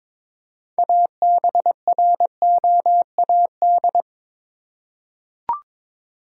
Morse code, heard as a keyed tone, about 21 wpm, 700 Hz. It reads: ABROAD
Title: Six Letter Words - 22wpm